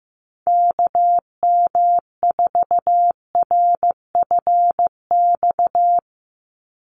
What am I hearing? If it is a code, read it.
Morse code, heard as a keyed tone, about 15 wpm, 700 Hz. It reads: KM4RFX
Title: USA Call Signs 15wpm